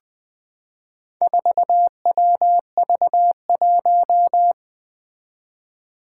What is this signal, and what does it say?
Morse code, heard as a keyed tone, about 20 wpm, 700 Hz. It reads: 4WV1
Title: Four Challenging Letter-Number Combinations 20wpm